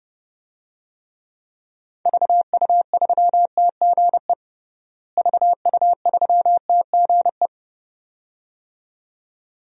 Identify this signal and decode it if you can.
Morse code, heard as a keyed tone, about 30 wpm, 700 Hz. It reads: VU3TGE VU3TGE
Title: DXCC-100 Call Signs - 6 Characters - Repeated 2x - 30wpm